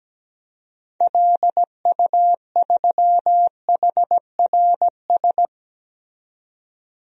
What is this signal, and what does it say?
Morse code, heard as a keyed tone, about 17 wpm, 700 Hz. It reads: LU3HRS
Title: DXCC-100 Call Signs - 6 Characters - 17wpm